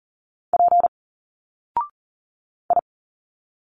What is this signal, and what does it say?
Morse code, heard as a keyed tone, about 40 wpm, 700 Hz. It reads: P I
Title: Single Letters 40wpm